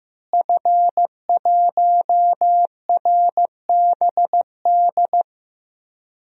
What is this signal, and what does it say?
Morse code, heard as a keyed tone, about 15 wpm, 700 Hz. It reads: F1RBD
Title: DXCC-100 Call Signs - 5 Characters - Repeated 2x - 15wpm